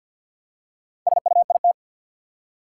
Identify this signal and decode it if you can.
Morse code, heard as a keyed tone, about 53 wpm, 700 Hz. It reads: SUIT
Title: Four Letter Words - 50wpm